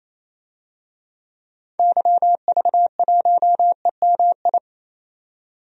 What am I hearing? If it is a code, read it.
Morse code, heard as a keyed tone, about 27 wpm, 700 Hz. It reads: YV1EMI
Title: DXCC-100 Call Signs - 6 Characters - 28wpm